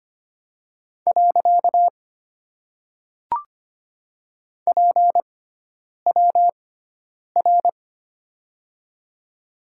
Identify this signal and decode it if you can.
Morse code, heard as a keyed tone, about 25 wpm, 700 Hz. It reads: . P W R
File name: Top 90 QSO Elements - 6x Character Spacing - 25wpm